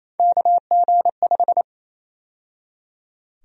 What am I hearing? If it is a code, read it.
Morse code, heard as a keyed tone, about 28 wpm, 700 Hz. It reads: KG5
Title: USA Call Sign Prefixes - 28wpm